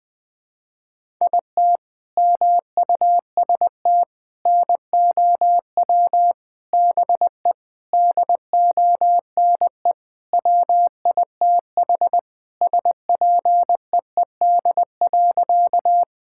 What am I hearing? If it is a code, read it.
Morse code, heard as a keyed tone, about 20 wpm, 700 Hz. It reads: IT MUST NOW BE DONE WITH SPEED.
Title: Sentences from Top 1000 Words - Review Part 5 - 20wpm